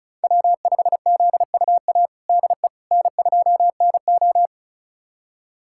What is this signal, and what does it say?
Morse code, heard as a keyed tone, about 36 wpm, 700 Hz. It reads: W5ZUA DE N2NO
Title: USA Callsign DE Callsign 35wpm